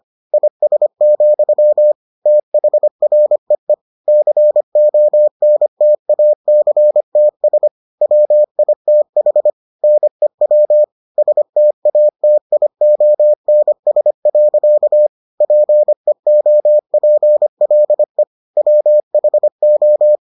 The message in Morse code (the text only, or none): IS, THREE CONTACTS WITH NEW STATIONS. PEOPLE WHO